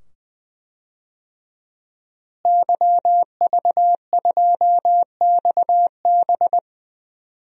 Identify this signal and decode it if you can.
Morse code, heard as a keyed tone, about 21 wpm, 700 Hz. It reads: YV2XB
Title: DXCC-100 Call Signs - 5 Characters - Repeated 2x - 20wpm